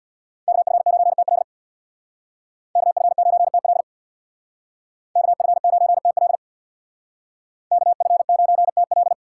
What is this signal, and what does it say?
Morse code, heard as a keyed tone, about 64 wpm, 700 Hz. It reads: KF9TL KF9TL KF9TL KF9TL